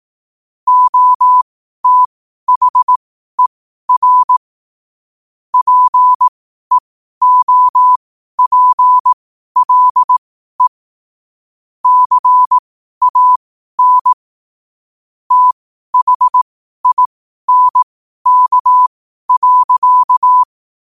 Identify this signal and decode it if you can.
Morse code, heard as a keyed tone, about 18 wpm, spaced as for 12 wpm, 1000 Hz. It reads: OTHER PEOPLE CAN THINK.
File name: qotdfiles / 12wpm